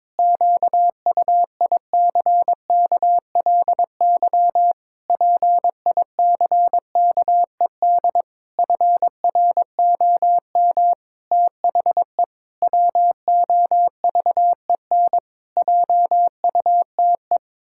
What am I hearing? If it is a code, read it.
Morse code, heard as a keyed tone, about 22 wpm, 700 Hz. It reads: QUICKLY PICKED FROM THE WOVEN JUTE
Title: Pangram Challenge v2 - 22wpm